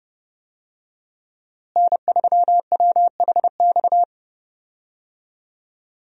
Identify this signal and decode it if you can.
Morse code, heard as a keyed tone, about 32 wpm, 700 Hz. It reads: N3WHX
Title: USA Call Signs - 5 Characters - Repeated 3x - 30wpm